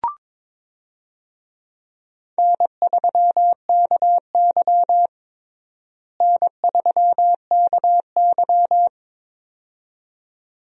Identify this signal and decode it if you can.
Morse code, heard as a keyed tone, about 22 wpm, 700 Hz. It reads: N3KY N3KY
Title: USA Call Signs - 4 Characters - Repeated 2x - 22wpm